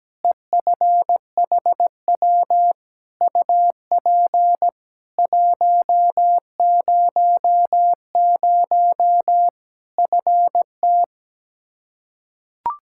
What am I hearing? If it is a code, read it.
Morse code, heard as a keyed tone, about 17 wpm, 700 Hz. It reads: EFHW UP 100 FT
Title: Practice QSO - Exchange 2 - Info - 17wpm